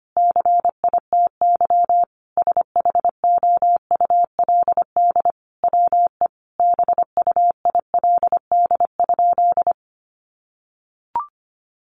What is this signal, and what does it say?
Morse code, heard as a keyed tone, about 25 wpm, 700 Hz. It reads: CITY SHOULD WE BUILD?
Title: Sentences from Top 1300 Words 25wpm